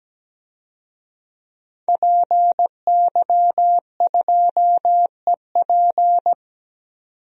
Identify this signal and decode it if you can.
Morse code, heard as a keyed tone, about 17 wpm, 700 Hz. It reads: PY2EP
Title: DXCC-100 Call Signs - 5 Characters - 17wpm